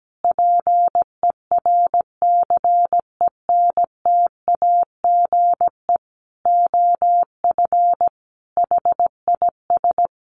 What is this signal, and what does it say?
Morse code, heard as a keyed tone, about 17 wpm, 700 Hz. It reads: PERCENTAGE OF HIS
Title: Sentences from Top 1300 Words 17wpm